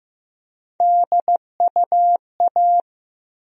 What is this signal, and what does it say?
Morse code, heard as a keyed tone, about 15 wpm, 700 Hz. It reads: DUA
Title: USA Call Sign Postfixes - 3 Letters 15wpm